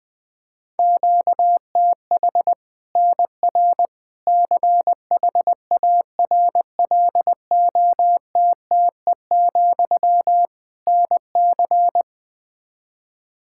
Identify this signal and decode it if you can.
Morse code, heard as a keyed tone, about 20 wpm, 700 Hz. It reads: QTH NR CHARLOTTE, NC
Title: Practice QSO - Exchange 1 - Beginning - 20wpm